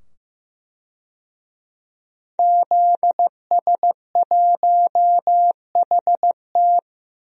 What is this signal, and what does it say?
Morse code, heard as a keyed tone, about 15 wpm, 700 Hz. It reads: ZS1HT
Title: DXCC-100 Call Signs - 5 Characters - 15wpm